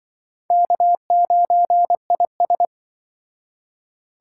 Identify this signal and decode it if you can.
Morse code, heard as a keyed tone, about 24 wpm, 700 Hz. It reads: K9IS
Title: Speed Racing - USA 1x2 Call Signs 20wpm